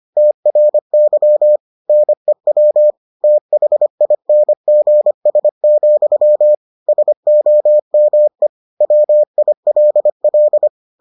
Morse code, heard as a keyed tone, about 25 wpm, 600 Hz. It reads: TRY NEW THINGS, SOME WILL